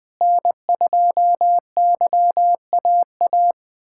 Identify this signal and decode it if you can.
Morse code, heard as a keyed tone, about 20 wpm, 700 Hz. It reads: N2YAA
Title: USA Callsign DE Callsign 20wpm